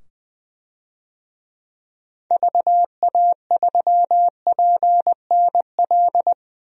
Morse code, heard as a keyed tone, about 20 wpm, 700 Hz. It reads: VA3PNL
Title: DXCC-100 Call Signs - 6 Characters - 20wpm